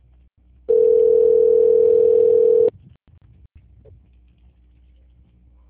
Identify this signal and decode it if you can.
Ringback tone (North America), 440 + 480 Hz.
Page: Bangu - RJ